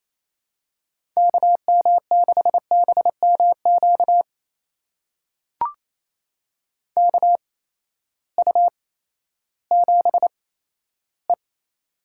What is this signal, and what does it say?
Morse code, heard as a keyed tone, about 28 wpm, spaced as for 14 wpm, 700 Hz. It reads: KM6BMQ KU7E
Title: USA Call Signs - 8x Character Spacing - 28wpm